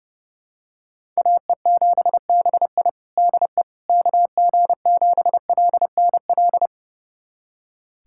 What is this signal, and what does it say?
Morse code, heard as a keyed tone, about 30 wpm, 700 Hz. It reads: AE7BI DE KG7LNL
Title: USA Callsign DE Callsign 30wpm